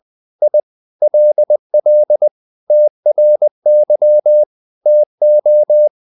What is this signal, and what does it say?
Morse code, heard as a keyed tone, about 20 wpm, 600 Hz. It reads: I LL TRY TO
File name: the old man and the sea / the old man and the sea part 1 20wpm